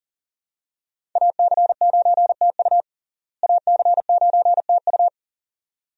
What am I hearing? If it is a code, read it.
Morse code, heard as a keyed tone, about 40 wpm, 700 Hz. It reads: AC9TU AC9TU
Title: USA Call Signs - 5 Characters - Repeated 2x - 40wpm